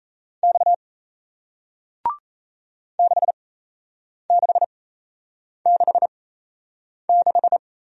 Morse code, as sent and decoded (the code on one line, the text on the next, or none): -..- / -.... -.... -.... -....
X 6666